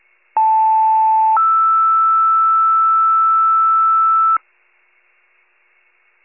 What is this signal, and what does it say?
Two-tone page, heard as a one-second tone, then a three-second tone, about 880 then 1320 Hz. It is heard from 0.4 to 4.4 s.
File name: wcalert